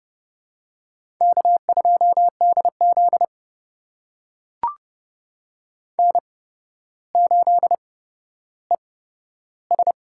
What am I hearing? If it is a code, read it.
Morse code, heard as a keyed tone, about 30 wpm, spaced as for 13 wpm, 700 Hz. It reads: K2DZ N8ES
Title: USA Call Signs - 4 Characters - 8x Character Spacing - 30wpm